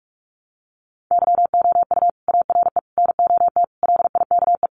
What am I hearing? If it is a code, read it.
Morse code, heard as a keyed tone, about 44 wpm, 700 Hz. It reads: YOU ARE NOT LIKE
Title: Sentences from Top 600 Words 45wpm